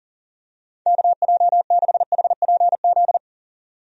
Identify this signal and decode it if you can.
Morse code, heard as a keyed tone, about 40 wpm, 700 Hz. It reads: KJ6HPZ
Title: USA Call Signs - 6 Characters - 40wpm